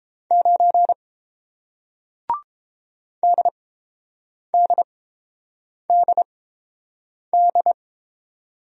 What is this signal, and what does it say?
Morse code, heard as a keyed tone, about 32 wpm, spaced as for 7 wpm, 700 Hz. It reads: 9 DDDD